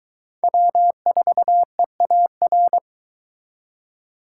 Morse code, heard as a keyed tone, about 22 wpm, 700 Hz. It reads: W4EAR